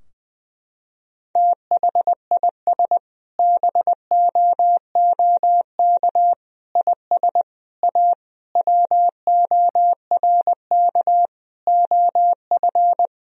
Morse code, heard as a keyed tone, about 20 wpm, 700 Hz. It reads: THIS BOOK IS A WORK OF
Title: Sentences from Top 1000 Words - Review Part 3 - 20wpm